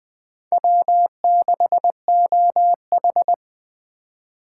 Morse code, heard as a keyed tone, about 20 wpm, 700 Hz. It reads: W6OH
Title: USA Call Signs - 4 Characters - 3x Character Spacing - 20wpm